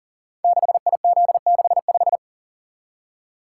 Morse code, heard as a keyed tone, about 41 wpm, 690 Hz. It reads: 6IZ65
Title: Code Groups - 5 Letter:Numbers - 40wpm